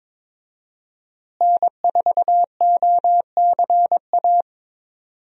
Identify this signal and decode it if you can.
Morse code, heard as a keyed tone, about 21 wpm, 700 Hz. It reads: N4OCA
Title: USA Call Signs - 5 Characters - Repeated 3x - 22wpm